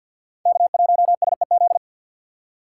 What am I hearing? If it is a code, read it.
Morse code, heard as a keyed tone, about 46 wpm, 700 Hz. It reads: K1SEZ
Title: USA Call Signs - 5 Characters - 2x Character Spacing - 50wpm